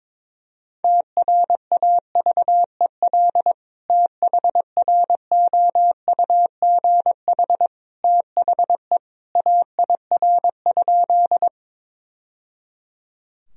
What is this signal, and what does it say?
Morse code, heard as a keyed tone, about 22 wpm, 700 Hz. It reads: TRAVEL THROUGH THE AIR?
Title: Sentences from Top 1000 Words - Review Part 2 - 22wpm